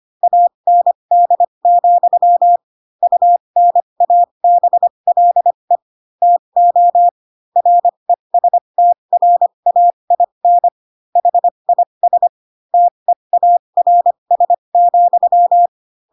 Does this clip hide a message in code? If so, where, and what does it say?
Yes: Morse code from 0.2 to 15.7 s, heard as a keyed tone, about 25 wpm, spaced as for 22 wpm, 700 Hz. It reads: AND, UNABLE TO RESTRAIN HIS TEARS,